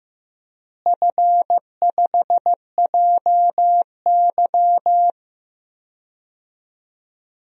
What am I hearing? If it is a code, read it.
Morse code, heard as a keyed tone, about 15 wpm, 700 Hz. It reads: F5JY
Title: DXCC-100 Call Signs - 4 Characters - Repeated 2x - 15wpm